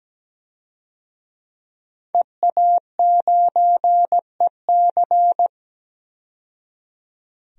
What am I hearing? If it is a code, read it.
Morse code, heard as a keyed tone, about 17 wpm, 700 Hz. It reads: EA9EC
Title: DXCC-100 Call Signs - 5 Characters - 17wpm